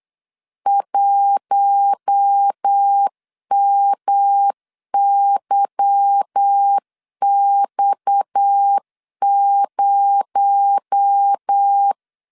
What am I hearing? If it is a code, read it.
Morse code, heard as a keyed tone, about 8 wpm, 800 Hz. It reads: 1MYX0
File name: LEZIONI / 10 SIGNS / 10A SLASH 8wpm 3452